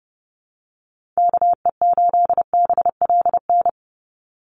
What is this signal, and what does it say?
Morse code, heard as a keyed tone, about 30 wpm, 700 Hz. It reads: KE8BLN
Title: USA Call Signs 30wpm